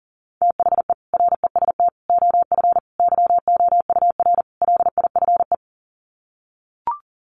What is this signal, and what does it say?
Morse code, heard as a keyed tone, about 40 wpm, 700 Hz. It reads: THE REST OF YOUR LIFE
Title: Sets of 5 Words 40wpm